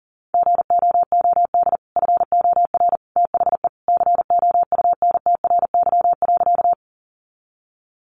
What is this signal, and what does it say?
Morse code, heard as a keyed tone, about 40 wpm, 700 Hz. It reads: GOOD FOR THE COUNTRY.
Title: Sentences from Top 300 Words 40wpm